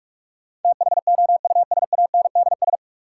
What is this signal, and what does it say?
Morse code, heard as a keyed tone, about 41 wpm, 700 Hz. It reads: THOUSANDS
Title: Sets of 4 Words - New Word Drill - 45wpm